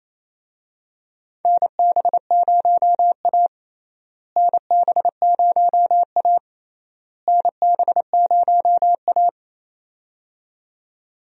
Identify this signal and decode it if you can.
Morse code, heard as a keyed tone, about 28 wpm, 700 Hz. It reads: NB0A NB0A NB0A
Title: USA Call Signs - 4 Characters - Repeated 3x - 28wpm